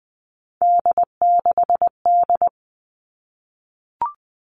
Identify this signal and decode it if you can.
Morse code, heard as a keyed tone, about 20 wpm, 700 Hz. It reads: D6D
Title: All Cut Numbers - 3-Digit 20wpm